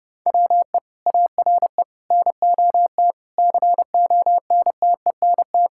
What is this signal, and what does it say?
Morse code, heard as a keyed tone, about 32 wpm, 700 Hz. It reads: WE ARE NOT CONTENT
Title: Sentences from Top 1000 Words - Review Part 6 - 30wpm